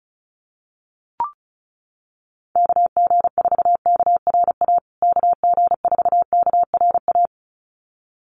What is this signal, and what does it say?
Morse code, heard as a keyed tone, about 34 wpm, 700 Hz. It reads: KG4KRA KG4KRA
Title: USA Call Signs Repeated 2x 35wpm